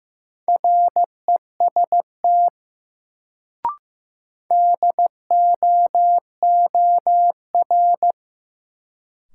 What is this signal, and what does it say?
Morse code, heard as a keyed tone, about 15 wpm, 700 Hz. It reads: REST DOOR
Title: Sets of 2 Words - New Word Drill - 15wpm